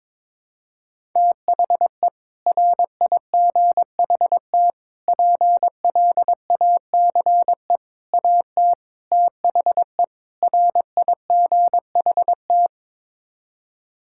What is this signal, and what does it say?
Morse code, heard as a keyed tone, about 21 wpm, 700 Hz. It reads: THE RIGHT PLACE AT THE RIGHT
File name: practice / 22wpm / Sets of 6 Words - 22wpm